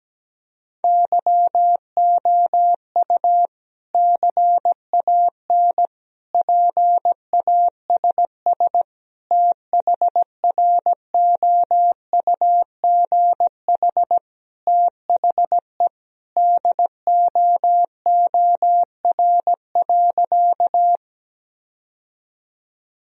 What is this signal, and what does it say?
Morse code, heard as a keyed tone, about 17 wpm, 700 Hz. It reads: YOU CAN PASS THROUGH THE DOOR.
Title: Sentences from Top 1000 Words - Review Part 1 - 17wpm